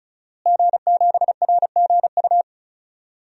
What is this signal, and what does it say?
Morse code, heard as a keyed tone, about 34 wpm, 700 Hz. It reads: G7RGU
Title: DXCC-100 Call Signs - 5 Characters - Repeated 3x - 35wpm